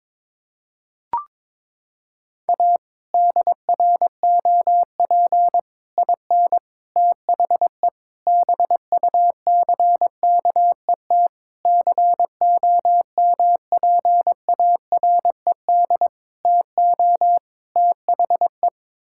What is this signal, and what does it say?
Morse code, heard as a keyed tone, about 22 wpm, 700 Hz. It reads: A DROP IN THE BUCKET COMPARED TO THE
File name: practice / 22wpm / Sets of 8 Words - 22wpm